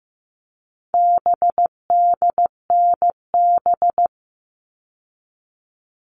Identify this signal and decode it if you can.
Morse code, heard as a keyed tone, about 15 wpm, 700 Hz. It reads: BDNB
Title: All Cut Numbers - 4-Digit 15wpm